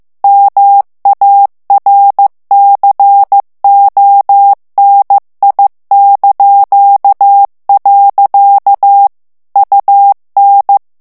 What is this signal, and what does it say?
Morse code, heard as a keyed tone, about 15 wpm, 800 Hz. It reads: MARCONI). UN
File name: ITALIEN texte clair 1 vitesse 14 mots minute